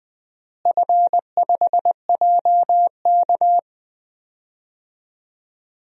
Morse code, heard as a keyed tone, about 20 wpm, 700 Hz. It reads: F5JK
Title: DXCC-100 Call Signs - 4 Characters - Repeated 3x - 20wpm